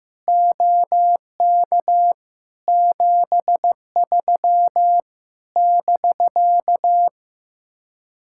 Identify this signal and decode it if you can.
Morse code, heard as a keyed tone, about 15 wpm, 700 Hz. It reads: OK 73 <BK>
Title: Speed-Racing Common QSO Phrases Encore - 15wpm